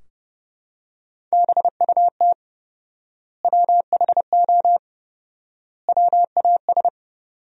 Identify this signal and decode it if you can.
Morse code, heard as a keyed tone, about 30 wpm, 700 Hz. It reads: BUT WHO WAS
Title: 4x Word-Spacing - Sentences from Top 100 Words - 30wpm